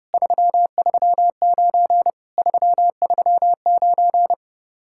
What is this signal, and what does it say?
Morse code, heard as a keyed tone, about 30 wpm, 700 Hz. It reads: 339 339